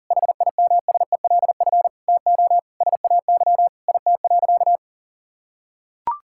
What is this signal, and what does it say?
Morse code, heard as a keyed tone, about 37 wpm, 700 Hz. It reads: HIMSELF TO SAY IT.
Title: Sentences from Top 1600 Words 40wpm